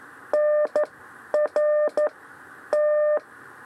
Morse code, heard as a keyed tone, about 12 wpm, spaced as for 10 wpm, 600 Hz. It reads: NR T
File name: 6 lezione cw 2